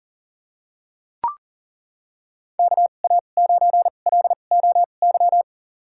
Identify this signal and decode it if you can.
Morse code, heard as a keyed tone, about 40 wpm, spaced as for 29 wpm, 700 Hz. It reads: KA9LOY
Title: USA Call Signs - 2x Character Spacing - 40wpm